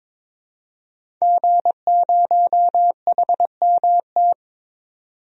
Morse code, heard as a keyed tone, about 22 wpm, 700 Hz. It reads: G0HMT
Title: DXCC-100 Call Signs - 5 Characters - 22wpm